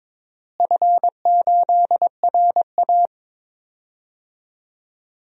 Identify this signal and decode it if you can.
Morse code, heard as a keyed tone, about 22 wpm, 700 Hz. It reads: F8RA
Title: DXCC-100 Call Signs - 4 Characters - Repeated 3x - 22wpm